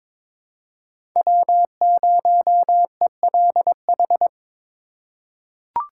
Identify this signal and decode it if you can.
Morse code, heard as a keyed tone, about 22 wpm, 700 Hz. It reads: W0ELH
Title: USA Call Signs - 5 Characters - 3x Character Spacing - 22wpm